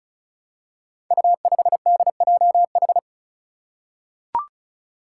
Morse code, heard as a keyed tone, about 36 wpm, 700 Hz. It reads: U5DJH
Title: Five Challenging Letter-Number Combinations 35wpm